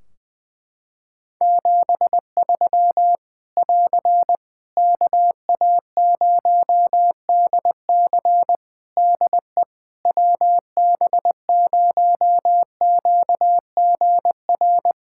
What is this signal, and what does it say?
Morse code, heard as a keyed tone, about 20 wpm, 700 Hz. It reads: 73 <AR> KA0DC DE WB0QGR
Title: Practice QSO - Exchange 4 - Ending - 20wpm